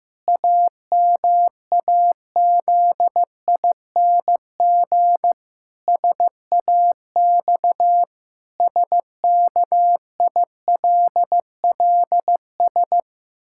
Morse code, heard as a keyed tone, about 15 wpm, 700 Hz. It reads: AMAZING SAX SKILLS
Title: Pangram Challenge v2 - 15wpm